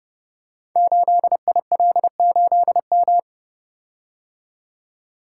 Morse code, heard as a keyed tone, about 29 wpm, 700 Hz. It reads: 8IL8M
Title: Code Groups - 5 Letter:Numbers - 30wpm